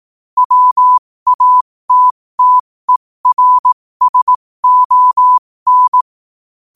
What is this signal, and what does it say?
Morse code, heard as a keyed tone, about 18 wpm, spaced as for 15 wpm, 1000 Hz. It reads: WATTERSON
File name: qotdfiles / 15wpm